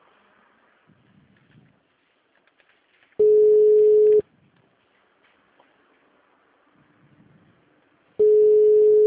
Ringback tone (Europe, 425 Hz).
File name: out- -2016- - - .12045